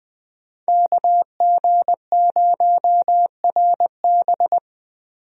Morse code, heard as a keyed tone, about 20 wpm, 700 Hz. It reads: KG0RB
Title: USA Call Signs - 5 Characters - Repeated 3x - 20wpm